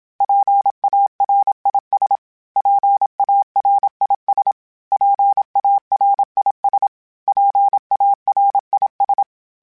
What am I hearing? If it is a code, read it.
Morse code, heard as a keyed tone, about 27 wpm, 800 Hz. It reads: PARIS PARIS PARIS PARIS
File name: PARIS Calibration / PARIS mp3s / PARIS@25